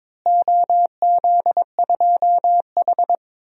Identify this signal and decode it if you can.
Morse code, heard as a keyed tone, about 22 wpm, 700 Hz. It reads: OZ2H
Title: DXCC-100 Call Signs - 4 Characters - Repeated 3x - 22wpm